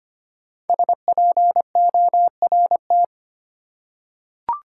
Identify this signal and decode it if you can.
Morse code, heard as a keyed tone, about 25 wpm, 700 Hz. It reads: SPORT
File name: 5 Letter Words - First 3 Letters Given - 25wpm